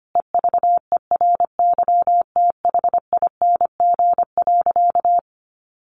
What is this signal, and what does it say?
Morse code, heard as a keyed tone, about 25 wpm, 700 Hz. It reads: EVERYTHING.
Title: Sentences from Top 1100 Words 25wpm